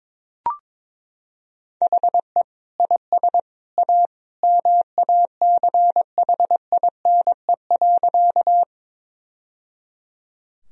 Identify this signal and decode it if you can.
Morse code, heard as a keyed tone, about 22 wpm, 700 Hz. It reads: HE IS A MACHINE.
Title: Sentences from Top 1000 Words - Review Part 2 - 22wpm